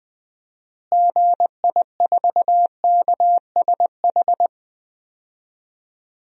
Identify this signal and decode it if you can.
Morse code, heard as a keyed tone, about 21 wpm, 700 Hz. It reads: GI4KSH
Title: DXCC-100 Call Signs - 6 Characters - Repeated 2x - 20wpm